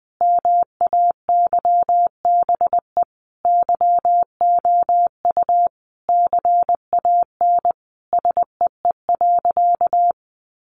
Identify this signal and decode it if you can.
Morse code, heard as a keyed tone, about 20 wpm, 700 Hz. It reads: MAYBE YOU CAN SEE.
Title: Sentences from Top 1000 Words 20wpm